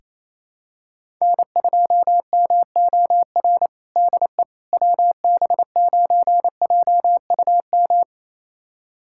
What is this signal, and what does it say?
Morse code, heard as a keyed tone, about 28 wpm, 700 Hz. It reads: N2MOR DE WB9JUM
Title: USA Callsign DE Callsign 28wpm